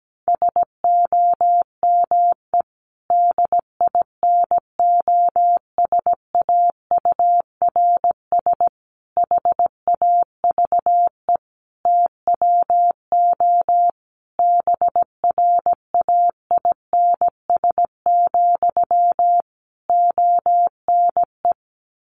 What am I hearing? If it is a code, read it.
Morse code, heard as a keyed tone, about 17 wpm, 700 Hz. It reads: SOME DINOSAURS HAVE TWO BRAINS, ONE
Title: A Sound Of Thunder 17wpm